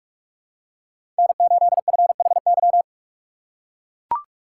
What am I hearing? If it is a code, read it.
Morse code, heard as a keyed tone, about 44 wpm, 690 Hz. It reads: N8FHY
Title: USA Call Signs - 5 Characters - 6x Character Spacing - 45wpm